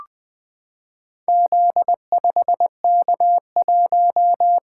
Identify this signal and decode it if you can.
Morse code, heard as a keyed tone, about 20 wpm, 700 Hz. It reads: Z5K1
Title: Four Letter-Number Combinations - 20wpm